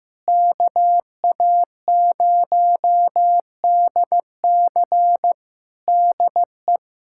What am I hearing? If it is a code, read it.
Morse code, heard as a keyed tone, about 15 wpm, 700 Hz. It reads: KA0DC DE
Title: K1USN SST QSO Practice 15wpm